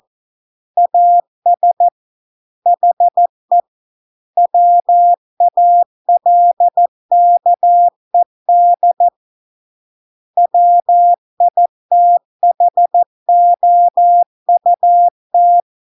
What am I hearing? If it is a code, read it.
Morse code, heard as a keyed tone, about 14 wpm, 700 Hz. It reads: AS HE WALKED WITHOUT